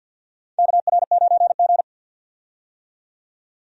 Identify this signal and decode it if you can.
Morse code, heard as a keyed tone, about 51 wpm, 700 Hz. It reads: KR9G
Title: USA Call Signs - 4 Characters - 50wpm